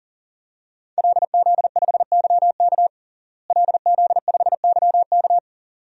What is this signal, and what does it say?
Morse code, heard as a keyed tone, about 37 wpm, 700 Hz. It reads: LZ5YK LZ5YK
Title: DXCC-100 Call Signs - 5 Characters - Repeated 2x - 40wpm